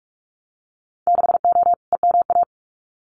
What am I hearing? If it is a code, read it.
Morse code, heard as a keyed tone, about 46 wpm, 700 Hz. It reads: 6O EMA
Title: Winter Field Day Exchanges Encore 45wpm